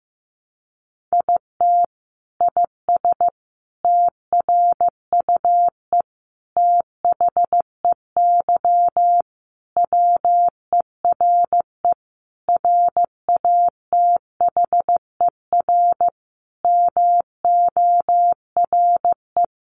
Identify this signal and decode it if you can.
Morse code, heard as a keyed tone, about 15 wpm, 700 Hz. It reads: IT IS TRUE THEY WERE RATHER MORE
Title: Sentences from Top 1400 Words 15wpm